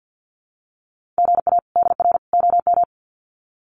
Morse code, heard as a keyed tone, about 48 wpm, 700 Hz. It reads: GA DR OM
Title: Common QSO Phrases 50wpm